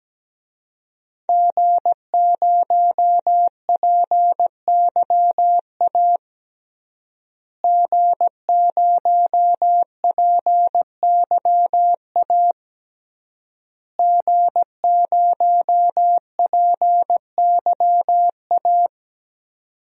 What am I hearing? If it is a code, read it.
Morse code, heard as a keyed tone, about 17 wpm, 700 Hz. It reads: G0PYA G0PYA G0PYA
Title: DXCC-100 Call Signs - 5 Characters - Repeated 3x - 17wpm